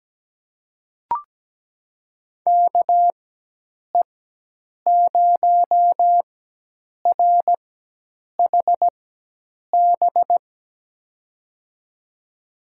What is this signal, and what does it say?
Morse code, heard as a keyed tone, about 17 wpm, spaced as for 8 wpm, 700 Hz. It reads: KE0RHB
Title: USA Call Signs - 6 Characters - 4x Character Spacing - 17wpm